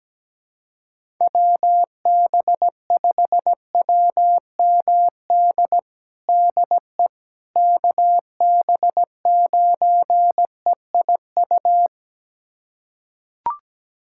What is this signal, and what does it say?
Morse code, heard as a keyed tone, about 17 wpm, 700 Hz. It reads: WB5WMD DE KB9EIU
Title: USA Callsign DE Callsign - 6 Characters - 17wpm